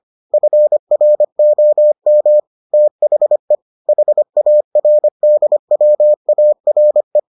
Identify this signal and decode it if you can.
Morse code, heard as a keyed tone, about 25 wpm, 600 Hz. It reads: FROM THE HARDWARE